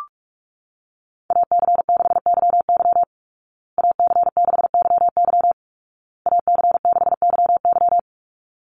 Morse code, heard as a keyed tone, about 44 wpm, 700 Hz. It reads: AC6YY AC6YY AC6YY
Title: Call Signs Repeated 2x - 45wpm